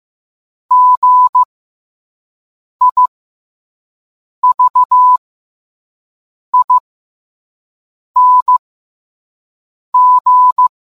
Morse code, heard as a keyed tone, about 15 wpm, spaced as for 5 wpm, 1000 Hz. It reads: GIVING